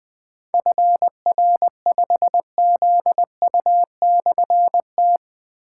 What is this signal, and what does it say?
Morse code, heard as a keyed tone, about 21 wpm, 700 Hz. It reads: FR5ZU/T
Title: Difficult Call Signs 20wpm